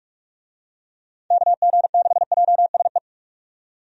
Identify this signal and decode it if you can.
Morse code, heard as a keyed tone, about 46 wpm, 700 Hz. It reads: KG6JSE
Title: USA Call Signs - 6 Characters - 45wpm